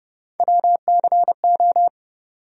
Morse code, heard as a keyed tone, about 30 wpm, 700 Hz. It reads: WCO
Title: USA Call Sign Postfixes - 3 Letters 30wpm